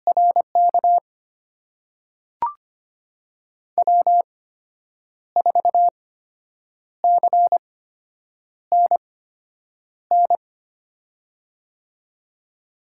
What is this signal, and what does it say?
Morse code, heard as a keyed tone, about 25 wpm, spaced as for 8 wpm, 700 Hz. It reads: RK W4CNN
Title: USA Call Signs - 5 Characters - 8x Character Spacing - 25wpm